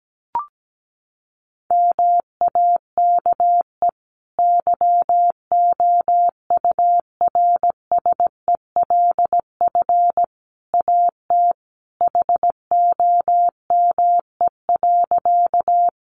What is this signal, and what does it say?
Morse code, heard as a keyed tone, about 17 wpm, 700 Hz. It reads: MAKE YOURSELF AT HOME.
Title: Sentences from Top 600 Words 17wpm